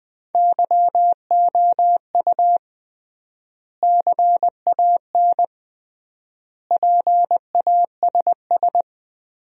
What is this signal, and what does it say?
Morse code, heard as a keyed tone, about 20 wpm, 700 Hz. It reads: YOU CAN PASS
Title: 3x Word-Spacing - Sentences from Top 400 Words - 20wpm